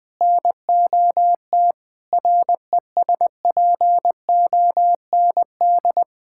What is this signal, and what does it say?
Morse code, heard as a keyed tone, about 21 wpm, 700 Hz. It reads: NOT RESPOND
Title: Sentences with 8-Word Phrases - 20wpm